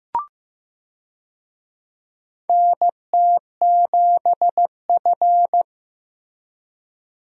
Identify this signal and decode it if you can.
Morse code, heard as a keyed tone, about 15 wpm, 700 Hz. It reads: NT7F
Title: USA Call Signs - 4 Characters - Repeated 3x - 15wpm